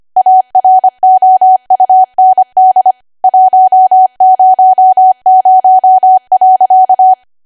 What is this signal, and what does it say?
Morse code, heard as a keyed tone, about 25 wpm, 750 Hz. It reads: AROUND 100.
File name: Archive / 25 WPM / 25WPM